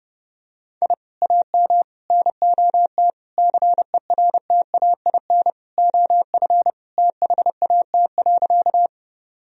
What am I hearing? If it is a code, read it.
Morse code, heard as a keyed tone, about 30 wpm, 700 Hz. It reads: I AM NOT CERTAIN OF THAT.
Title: Sentences from Top 1000 Words - Review Part 2 - 30wpm